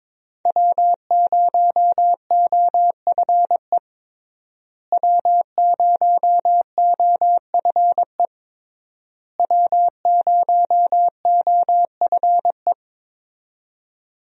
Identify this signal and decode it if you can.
Morse code, heard as a keyed tone, about 22 wpm, 700 Hz. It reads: W0OFE W0OFE W0OFE
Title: USA Call Signs - 5 Characters - Repeated 3x - 22wpm